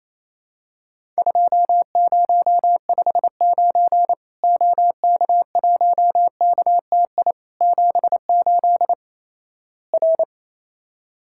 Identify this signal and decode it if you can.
Morse code, heard as a keyed tone, about 28 wpm, 700 Hz. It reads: 2059 OK1KTI 78 R
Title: Worked All Europe DX Contest - QTC Exchange - 28wpm